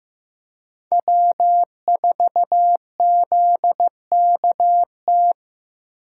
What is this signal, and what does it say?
Morse code, heard as a keyed tone, about 15 wpm, 700 Hz. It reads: W4ZKT
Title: USA Call Signs - 5 Characters - 8x Character Spacing - 15wpm